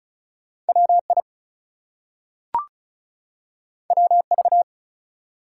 Morse code, heard as a keyed tone, about 36 wpm, 700 Hz. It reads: WI WV
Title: US States and Canadian Provinces 35wpm